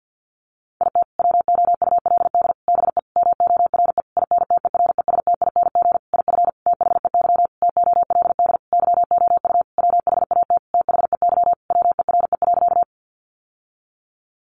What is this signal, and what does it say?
Morse code, heard as a keyed tone, about 50 wpm, 700 Hz. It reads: IT WOULD BE MORE INTERESTING IF THEY TOLD YOU WHAT THEY WERE.
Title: Sentences from Top 1000 Words 50wpm